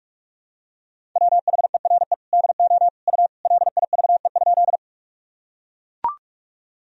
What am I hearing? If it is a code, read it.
Morse code, heard as a keyed tone, about 43 wpm, 700 Hz. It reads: WHERE DO U LIVE?
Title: Sentences from Top 200 Words and Abbreviations 45wpm